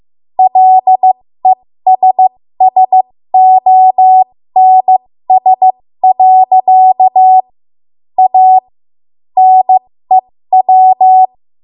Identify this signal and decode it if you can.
Morse code, heard as a keyed tone, about 15 wpm, spaced as for 13 wpm, 750 Hz. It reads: LESSONS. A NEW
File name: Morse / 13WPM